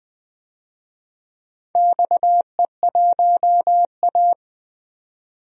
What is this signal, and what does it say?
Morse code, heard as a keyed tone, about 20 wpm, 700 Hz. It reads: XE1A
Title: DXCC-100 Call Signs - 4 Characters - 20wpm